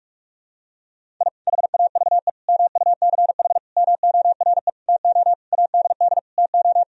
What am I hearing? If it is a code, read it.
Morse code, heard as a keyed tone, about 46 wpm, 700 Hz. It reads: I HAVE MUCH MORE TO ADD TO
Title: Sentences from Top 1000 Words - Review Part 1 - 45wpm